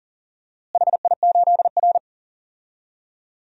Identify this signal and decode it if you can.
Morse code, heard as a keyed tone, about 40 wpm, 700 Hz. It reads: HI8R
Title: DXCC-100 Call Signs - 4 Characters - Repeated 2x - 40wpm